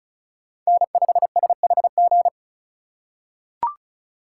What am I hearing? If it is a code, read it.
Morse code, heard as a keyed tone, about 36 wpm, 700 Hz. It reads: N5SHG
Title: USA Call Signs - 5 Characters - 2x Character Spacing - 35wpm